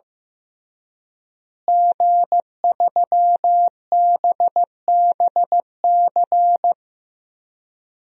Morse code, heard as a keyed tone, about 16 wpm, 700 Hz. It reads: G3BBC
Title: DXCC-100 Call Signs - 5 Characters - Repeated 3x - 15wpm